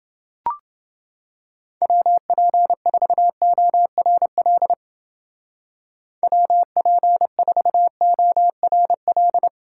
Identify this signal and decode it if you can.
Morse code, heard as a keyed tone, about 28 wpm, 700 Hz. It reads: WP4ORL WP4ORL